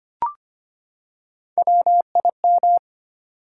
Morse code, heard as a keyed tone, about 25 wpm, 700 Hz. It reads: WIM
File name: Three Letter-Number Combinations - 25wpm